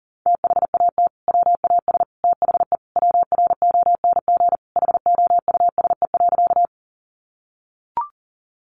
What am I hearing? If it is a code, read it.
Morse code, heard as a keyed tone, about 40 wpm, 700 Hz. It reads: THAT WAS THE WRONG HOUSE.
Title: Sentences from Top 900 Words 40wpm